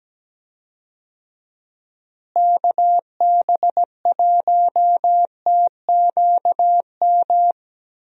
Morse code, heard as a keyed tone, about 17 wpm, 700 Hz. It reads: KB1TQM
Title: USA Call Signs - 6 Characters - Repeated 3x - 17wpm